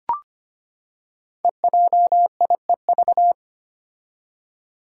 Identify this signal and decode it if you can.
Morse code, heard as a keyed tone, about 25 wpm, 700 Hz. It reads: EJIEV
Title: Five Challenging Letter-Number Combinations 25wpm